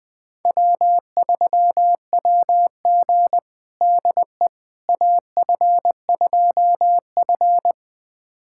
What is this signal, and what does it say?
Morse code, heard as a keyed tone, about 20 wpm, 700 Hz. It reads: W3WG DE AF2F
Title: USA Callsign DE Callsign - 4 Characters - 20wpm